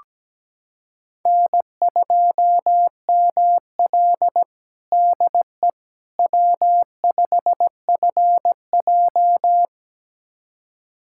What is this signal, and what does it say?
Morse code, heard as a keyed tone, about 17 wpm, 700 Hz. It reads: N2ML DE W5FJ
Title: USA Callsign DE Callsign - 4 Characters - 17wpm